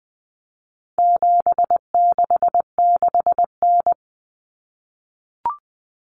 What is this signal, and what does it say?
Morse code, heard as a keyed tone, about 20 wpm, 700 Hz. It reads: 766N
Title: Common Cut Numbers - 4-Digit 20wpm